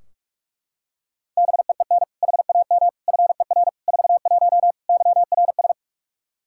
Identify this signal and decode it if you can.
Morse code, heard as a keyed tone, about 41 wpm, 700 Hz. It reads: BEEN HAM FER 41 YRS
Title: Practice QSO - Exchange 3 - More Info - 45wpm